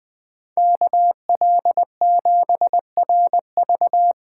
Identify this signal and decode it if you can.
Morse code, heard as a keyed tone, about 20 wpm, 700 Hz. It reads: KL7RV
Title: USA Call Signs - 4x Character Spacing - 20wpm